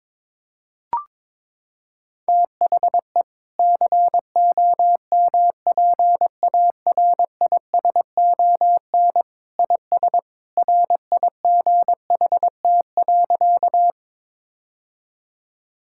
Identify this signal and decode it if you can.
Morse code, heard as a keyed tone, about 22 wpm, 700 Hz. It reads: THE COMPARISON IS RIGHT.
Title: Sentences from Top 1600 Words 22wpm